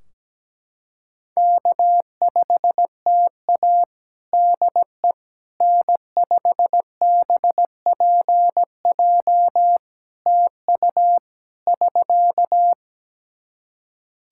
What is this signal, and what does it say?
Morse code, heard as a keyed tone, about 17 wpm, 700 Hz. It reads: K5TA DE N5BPJ TU <SK>
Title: K1USN SST QSO Practice 17wpm